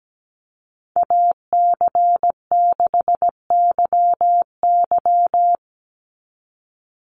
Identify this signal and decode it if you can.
Morse code, heard as a keyed tone, about 17 wpm, 700 Hz. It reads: AC6YY
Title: Call Signs - 17wpm